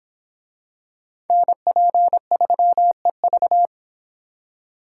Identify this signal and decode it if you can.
Morse code, heard as a keyed tone, about 23 wpm, 700 Hz. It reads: NP3EV